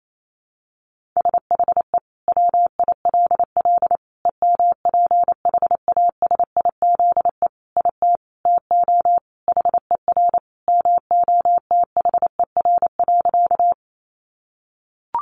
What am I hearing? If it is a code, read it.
Morse code, heard as a keyed tone, about 28 wpm, 700 Hz. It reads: SHE WILL EMPHASIZE IT TO HER MOTHER.
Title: Sentences from Top 1500 Words 28wpm